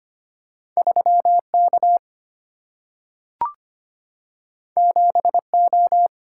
Morse code, heard as a keyed tone, about 25 wpm, 700 Hz. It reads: 3K 7O
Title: Single-Letter Single-Number 25wpm